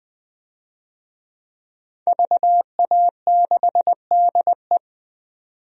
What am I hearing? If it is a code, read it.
Morse code, heard as a keyed tone, about 20 wpm, 700 Hz. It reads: VA6DE
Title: DXCC-100 Call Signs - 5 Characters - 20wpm